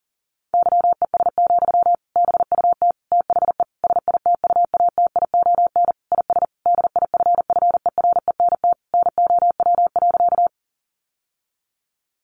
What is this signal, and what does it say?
Morse code, heard as a keyed tone, about 40 wpm, 700 Hz. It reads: YES, BUT THE SITUATION IS DIFFERENT NOW.
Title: Sentences from Top 1200 Words 40wpm